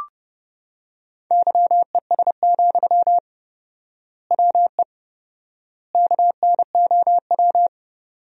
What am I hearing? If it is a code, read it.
Morse code, heard as a keyed tone, about 30 wpm, 700 Hz. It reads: YES, WE KNOW
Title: 4x Word-Spacing - Sentences from Top 500 Words - 30wpm